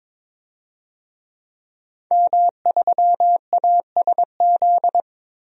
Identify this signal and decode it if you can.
Morse code, heard as a keyed tone, about 22 wpm, 700 Hz. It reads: M3ASZ
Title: DXCC-100 Call Signs - 5 Characters - 22wpm